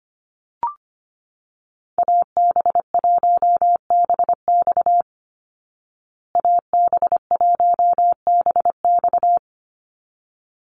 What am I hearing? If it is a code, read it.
Morse code, heard as a keyed tone, about 25 wpm, 700 Hz. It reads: AB1BX AB1BX
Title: Call Signs Repeated 3x - 25wpm